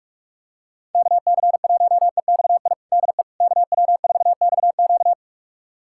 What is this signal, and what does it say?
Morse code, heard as a keyed tone, about 46 wpm, 690 Hz. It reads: KC1EXI DE KW4XQ
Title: USA Callsign DE Callsign 45wpm